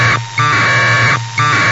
Packet radio (AX.25, as in APRS).
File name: Sohla-1 1200bd packet HiMode by ja0caw